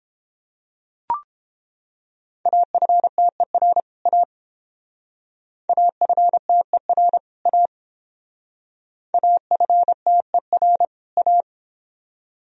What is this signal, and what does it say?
Morse code, heard as a keyed tone, about 30 wpm, 700 Hz. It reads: AFTER A AFTER A AFTER A